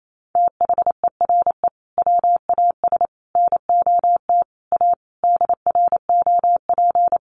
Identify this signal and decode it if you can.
Morse code, heard as a keyed tone, about 28 wpm, 700 Hz. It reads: THERE WAS NOT A DROP
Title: Sentences from Top 1400 Words 28wpm